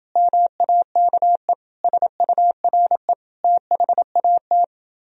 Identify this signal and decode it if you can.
Morse code, heard as a keyed tone, about 30 wpm, 700 Hz. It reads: MAKE SURE THAT